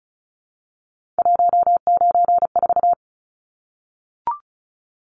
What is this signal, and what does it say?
Morse code, heard as a keyed tone, about 34 wpm, 700 Hz. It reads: 194
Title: Numbers - 3-Digit 35wpm